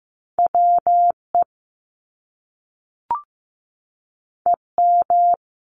Morse code, heard as a keyed tone, about 15 wpm, 700 Hz. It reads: WE EM
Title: Top Two Letter Combinations in Words 15wpm